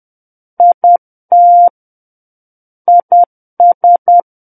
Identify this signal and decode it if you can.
Morse code, heard as a keyed tone, about 10 wpm, 700 Hz. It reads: IT IS
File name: Book One / 01-Chapter 1 - 10 WPM